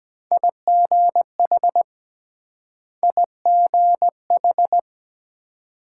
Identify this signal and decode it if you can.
Morse code, heard as a keyed tone, about 23 wpm, 700 Hz. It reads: IGH IGH